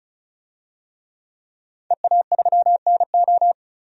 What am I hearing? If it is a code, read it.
Morse code, heard as a keyed tone, about 36 wpm, 700 Hz. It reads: EA3NO
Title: DXCC-100 Call Signs - 5 Characters - Repeated 2x - 35wpm